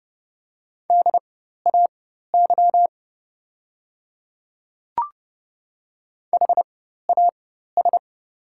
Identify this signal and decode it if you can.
Morse code, heard as a keyed tone, about 30 wpm, spaced as for 14 wpm, 700 Hz. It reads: DAY HAS
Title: 4x Character Spacing - Top 100 Words - 30wpm